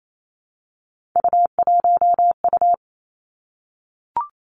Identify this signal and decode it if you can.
Morse code, heard as a keyed tone, about 28 wpm, 700 Hz. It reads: U1U
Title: Three Challenging Letter-Number Combinations 28wpm